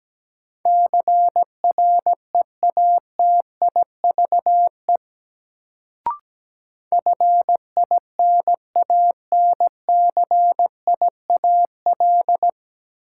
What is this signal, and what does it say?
Morse code, heard as a keyed tone, about 17 wpm, 700 Hz. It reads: CREATIVE FINANCIAL
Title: Sentences from Top 1000 Words - Review Part 5 - New Word Drill - 17wpm